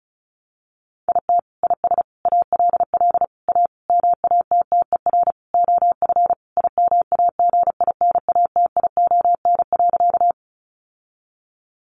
Morse code, heard as a keyed tone, about 34 wpm, 700 Hz. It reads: IT IS ALL A MATTER OF IMAGINATION.
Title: Sentences from Top 1300 Words 35wpm